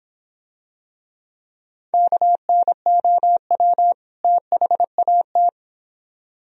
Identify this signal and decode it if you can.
Morse code, heard as a keyed tone, about 26 wpm, 700 Hz. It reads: KNOW THAT